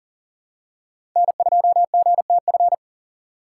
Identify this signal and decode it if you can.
Morse code, heard as a keyed tone, about 40 wpm, 700 Hz. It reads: N2GTF
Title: USA Call Signs - 5 Characters - 4x Character Spacing - 40wpm